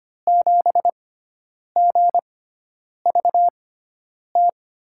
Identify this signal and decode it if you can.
Morse code, heard as a keyed tone, about 25 wpm, spaced as for 9 wpm, 700 Hz. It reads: 7GVT
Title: USA Call Signs - 6x Character Spacing - 25wpm